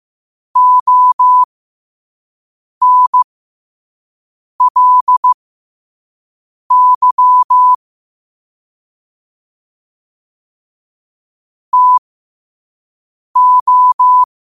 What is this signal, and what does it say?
Morse code, heard as a keyed tone, about 15 wpm, spaced as for 5 wpm, 1000 Hz. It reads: ONLY TO
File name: qotdfiles / 5wpm